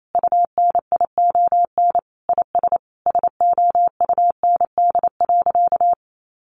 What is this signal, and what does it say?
Morse code, heard as a keyed tone, about 28 wpm, 700 Hz. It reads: UNION IS SOUND.
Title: Sentences from Top 1400 Words 28wpm